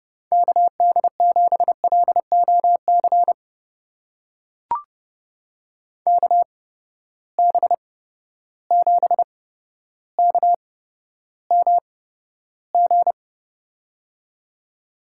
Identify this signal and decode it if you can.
Morse code, heard as a keyed tone, about 30 wpm, spaced as for 13 wpm, 700 Hz. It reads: KD7LOC KB7KMG
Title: USA Call Signs - 6 Characters - 8x Character Spacing - 30wpm